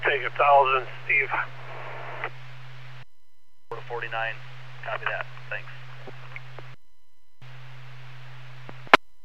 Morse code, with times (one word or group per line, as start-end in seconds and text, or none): none